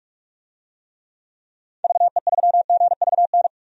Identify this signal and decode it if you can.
Morse code, heard as a keyed tone, about 46 wpm, 700 Hz. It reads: VE3GVN
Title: DXCC-100 Call Signs - 6 Characters - 45wpm